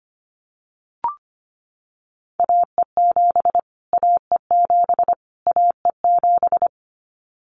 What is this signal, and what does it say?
Morse code, heard as a keyed tone, about 25 wpm, 700 Hz. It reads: AE7 AE7 AE7
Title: Callsign Prefixes for Top 25 DXCC - Repeated 3x - 25wpm